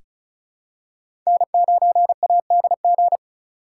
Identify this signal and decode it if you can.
Morse code, heard as a keyed tone, about 36 wpm, 700 Hz. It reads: N9ADG
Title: K1USN SST QSO Practice - 35wpm